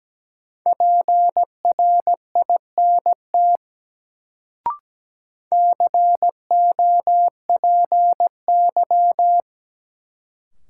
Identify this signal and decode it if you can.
Morse code, heard as a keyed tone, about 17 wpm, 700 Hz. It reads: PRINT COPY
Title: Sentences from Top 1400 Words - New Word Drill Alt - 17wpm